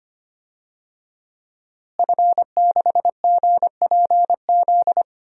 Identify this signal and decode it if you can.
Morse code, heard as a keyed tone, about 25 wpm, 700 Hz. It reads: F6GPZ
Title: DXCC-100 Call Signs - 5 Characters - Repeated 3x - 25wpm